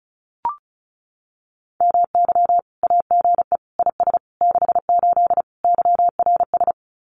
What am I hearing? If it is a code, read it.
Morse code, heard as a keyed tone, about 34 wpm, 700 Hz. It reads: MY AGE IS 68 YRS
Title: Common QSO Phrases Encore 35wpm